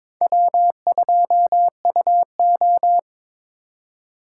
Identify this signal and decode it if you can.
Morse code, heard as a keyed tone, about 23 wpm, 700 Hz. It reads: W2UO
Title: USA Call Signs - 4 Characters - 4x Character Spacing - 22wpm